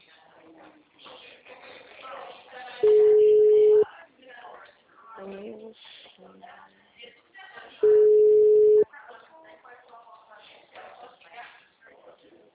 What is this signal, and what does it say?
Ringback tone (Europe, 425 Hz), with speech.
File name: out- -2016- - - .738